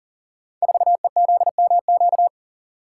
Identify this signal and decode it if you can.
Morse code, heard as a keyed tone, about 41 wpm, 700 Hz. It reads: 4EZMQ
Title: Code Groups - 5 Letter:Numbers - 40wpm